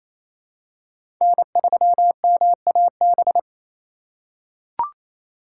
Morse code, heard as a keyed tone, about 28 wpm, 700 Hz. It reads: N3MAB
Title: USA Call Signs - 5 Characters - 2x Character Spacing - 28wpm